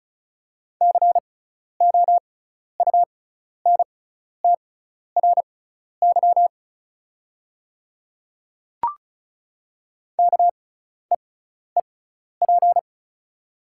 Morse code, heard as a keyed tone, about 34 wpm, spaced as for 12 wpm, 700 Hz. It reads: COUNTRY KEEP
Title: 6x Character Spacing - Top 300 Words - 35wpm